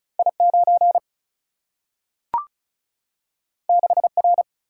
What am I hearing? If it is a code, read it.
Morse code, heard as a keyed tone, about 36 wpm, 700 Hz. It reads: I 9 6 R
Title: Single-Letter Single-Number 35wpm